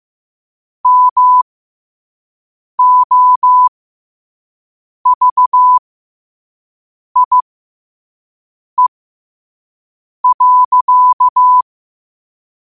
Morse code, heard as a keyed tone, about 15 wpm, spaced as for 5 wpm, 1000 Hz. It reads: MOVIE.
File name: qotdfiles / 5wpm